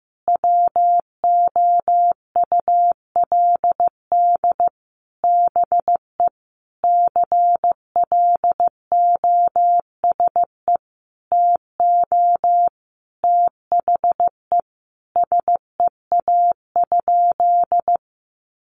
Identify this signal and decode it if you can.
Morse code, heard as a keyed tone, about 15 wpm, 700 Hz. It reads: WOULD BE CLOSE TO THE SEA?
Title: Sentences from Top 300 Words 15wpm